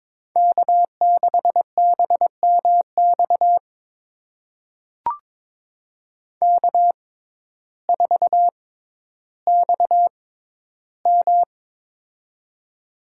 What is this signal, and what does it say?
Morse code, heard as a keyed tone, about 22 wpm, spaced as for 12 wpm, 700 Hz. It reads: K6BMX K4XM
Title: USA Call Signs - 6x Character Spacing - 22wpm